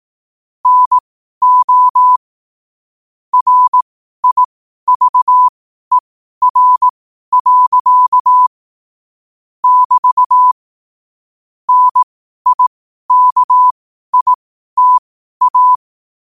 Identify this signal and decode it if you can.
Morse code, heard as a keyed tone, about 18 wpm, spaced as for 12 wpm, 1000 Hz. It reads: NO RIVER. <BT> NIKITA
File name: qotdfiles / 12wpm